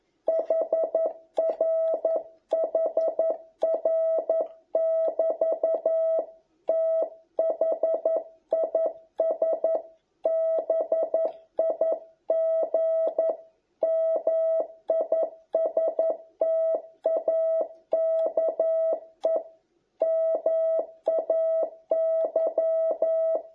Morse code, heard as a keyed tone, about 11 wpm, 660 Hz. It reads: HR HR <BT> THIS BIG MISTAKE MAY